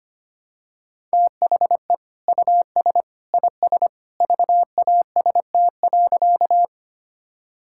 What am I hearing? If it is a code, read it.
Morse code, heard as a keyed tone, about 25 wpm, 700 Hz. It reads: THE US IS VAST.
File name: practice / 25wpm / Sentences with 2-Word Phrases - 25wpm